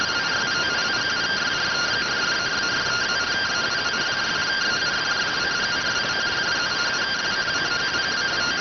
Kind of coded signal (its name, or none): two-tone teleprinter signal